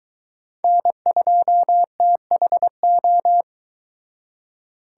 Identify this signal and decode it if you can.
Morse code, heard as a keyed tone, about 22 wpm, 700 Hz. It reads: N2THO